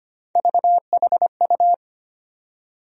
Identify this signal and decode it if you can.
Morse code, heard as a keyed tone, about 25 wpm, 700 Hz. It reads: VHU
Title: USA Call Sign Postfixes - 3 Letters 25wpm